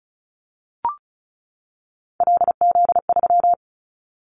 Morse code, heard as a keyed tone, about 36 wpm, 700 Hz. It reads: LZ3
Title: Callsign Prefixes for Top 100 DXCC - 35wpm